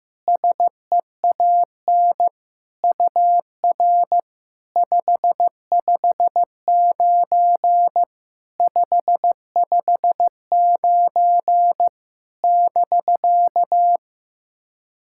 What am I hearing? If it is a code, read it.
Morse code, heard as a keyed tone, about 15 wpm, 700 Hz. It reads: SEAN UR 559 559 <BK>
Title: SOTA QSO Practice 15wpm